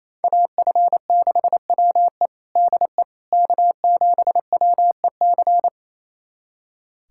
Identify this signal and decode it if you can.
Morse code, heard as a keyed tone, about 28 wpm, 700 Hz. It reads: AF6WE DE K7WEC
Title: USA Callsign DE Callsign - 5 Characters - 28wpm